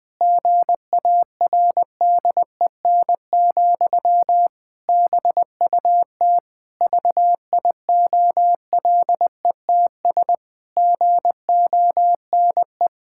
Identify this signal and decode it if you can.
Morse code, heard as a keyed tone, about 21 wpm, 700 Hz. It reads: GARDEN, BUT VIOLETS GONE
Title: Pangram Challenge v2 - 20wpm